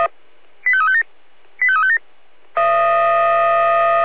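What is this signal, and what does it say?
Five-tone selective call (ZVEI), heard as quick runs of five tones, tones of about 70 ms. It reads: 85357 85357
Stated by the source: ZVEI five-tone selective call 85213, received by SDR.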